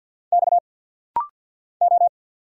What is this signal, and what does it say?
Morse code, heard as a keyed tone, about 50 wpm, 690 Hz. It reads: X O
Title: Single Letter-Number Alt - Rapid-Fire 50wpm